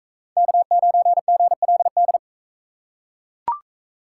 Morse code, heard as a keyed tone, about 39 wpm, 700 Hz. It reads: K9GLD